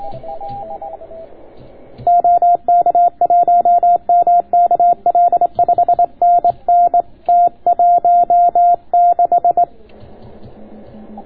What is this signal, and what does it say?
Morse code, heard as a keyed tone, about 25 wpm, 690 Hz. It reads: OK1MKL 5NN T16